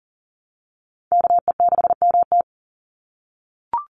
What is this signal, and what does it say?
Morse code, heard as a keyed tone, about 40 wpm, 700 Hz. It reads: KE6MT
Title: NA SOTA Chaser Callsigns 40wpm